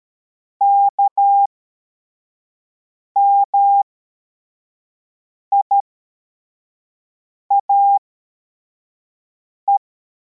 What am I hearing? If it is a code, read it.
Morse code, heard as a keyed tone, about 13 wpm, spaced as for 4 wpm, 800 Hz. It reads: KMIAE